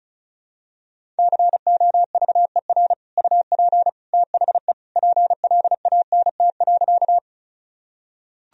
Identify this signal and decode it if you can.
Morse code, heard as a keyed tone, about 36 wpm, 700 Hz. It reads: COVER UP THE PLANT.
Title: Sentences from Top 1000 Words - Review Part 1 - 35wpm